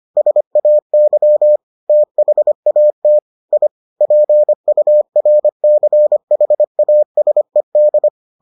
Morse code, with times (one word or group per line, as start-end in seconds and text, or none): 0.2-1.6 SAY
1.9-3.2 THAT
3.5-3.7 I
4.0-8.1 PURCHASED